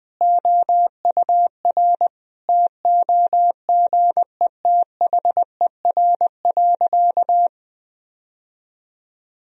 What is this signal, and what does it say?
Morse code, heard as a keyed tone, about 19 wpm, 700 Hz. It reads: OUR TOGETHER.
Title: Sentences from Top 1000 Words - Review Part 1 - 20wpm